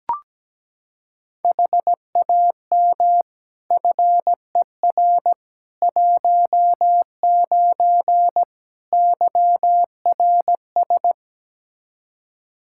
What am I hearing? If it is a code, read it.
Morse code, heard as a keyed tone, about 17 wpm, 700 Hz. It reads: HAM FER 19 YRS
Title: Practice QSO - Exchange 3 - More Info - 17wpm